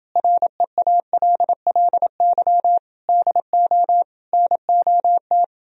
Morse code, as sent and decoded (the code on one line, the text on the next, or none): .-. .- .-.. .-.. -.-- / -.. --- / -. --- -
REALLY DO NOT